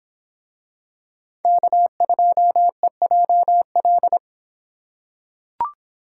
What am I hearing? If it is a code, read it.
Morse code, heard as a keyed tone, about 26 wpm, 700 Hz. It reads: K2EJL